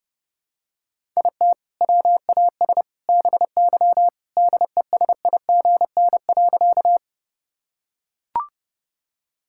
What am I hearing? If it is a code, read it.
Morse code, heard as a keyed tone, about 30 wpm, 700 Hz. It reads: IT WAS BY DESIGN.
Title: Sentences from Top 1000 Words - Review Part 5 - 30wpm